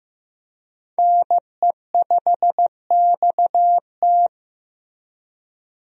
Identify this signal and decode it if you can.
Morse code, heard as a keyed tone, about 15 wpm, 700 Hz. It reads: NE5XT
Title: USA Call Signs - 5 Characters - Repeated 3x - 15wpm